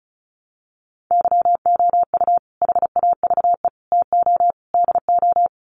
Morse code, heard as a keyed tone, about 34 wpm, 700 Hz. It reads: YOU HAVE TO DO
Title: Sets of 4 Words 35wpm